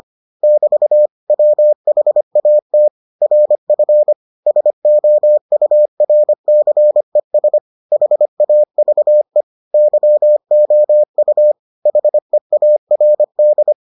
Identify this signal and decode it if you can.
Morse code, heard as a keyed tone, about 25 wpm, 600 Hz. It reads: <BT> WHAT RF SOURCES HAVE YOU HEARD